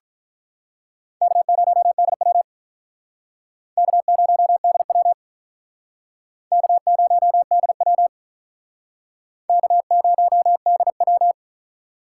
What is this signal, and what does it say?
Morse code, heard as a keyed tone, about 47 wpm, 700 Hz. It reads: K0DW K0DW K0DW K0DW